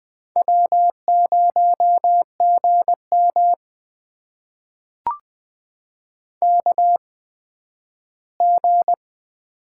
Morse code, heard as a keyed tone, about 20 wpm, spaced as for 12 wpm, 700 Hz. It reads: W0GM KG
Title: USA Call Signs - 8x Character Spacing - 20wpm